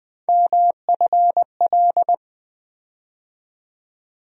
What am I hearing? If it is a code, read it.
Morse code, heard as a keyed tone, about 20 wpm, 700 Hz. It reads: MFL
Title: USA Call Sign Postfixes - 3 Letters 20wpm